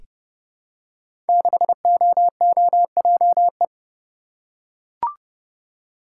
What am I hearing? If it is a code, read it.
Morse code, heard as a keyed tone, about 28 wpm, 700 Hz. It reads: 6OOJE
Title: Code Groups - 5 Letter:Numbers - 30wpm